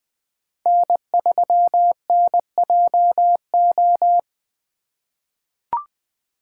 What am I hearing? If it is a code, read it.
Morse code, heard as a keyed tone, about 20 wpm, 700 Hz. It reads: N3NJO
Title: USA Call Signs - 5 Characters - 4x Character Spacing - 20wpm